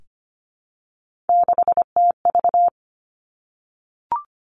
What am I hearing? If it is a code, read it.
Morse code, heard as a keyed tone, about 25 wpm, 700 Hz. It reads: 6TV
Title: Three Challenging Letter-Number Combinations 25wpm